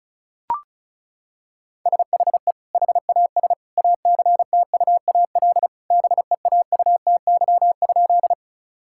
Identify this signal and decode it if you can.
Morse code, heard as a keyed tone, about 34 wpm, 700 Hz. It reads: SHE HAS ACTUAL BEAUTY?
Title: Sentences from Top 1000 Words - Review Part 5 - 35wpm